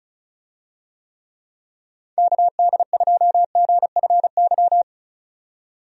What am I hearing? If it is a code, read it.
Morse code, heard as a keyed tone, about 36 wpm, 700 Hz. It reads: KD2GFY
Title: USA Call Signs - 6 Characters - 35wpm